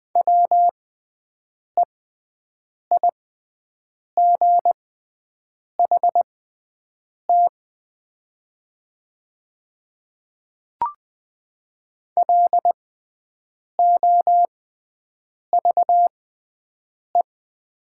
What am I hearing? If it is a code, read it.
Morse code, heard as a keyed tone, about 20 wpm, spaced as for 7 wpm, 700 Hz. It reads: WEIGHT LOVE
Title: 6x Character Spacing - Top 500 Words - 20wpm